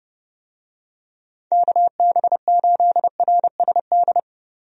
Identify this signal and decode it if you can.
Morse code, heard as a keyed tone, about 31 wpm, 700 Hz. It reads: KB8RSD
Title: USA Call Signs - 6 Characters - 30wpm